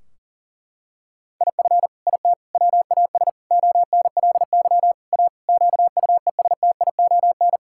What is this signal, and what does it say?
Morse code, heard as a keyed tone, about 40 wpm, 700 Hz. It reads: IF IT WAS ONLY A QUESTION
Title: Sentences from Top 1600 Words 40wpm